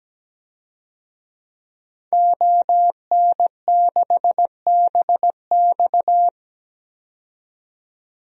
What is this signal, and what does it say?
Morse code, heard as a keyed tone, about 17 wpm, 700 Hz. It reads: ON6BX
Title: DXCC-100 Call Signs - 5 Characters - 17wpm